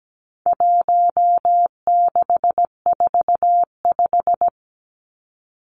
Morse code, heard as a keyed tone, about 17 wpm, 700 Hz. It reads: 1645
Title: Numbers - 4-Digit 17wpm